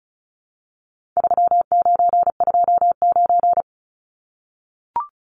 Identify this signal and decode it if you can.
Morse code, heard as a keyed tone, about 34 wpm, 700 Hz. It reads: 3929
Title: Numbers - 4-Digit 35wpm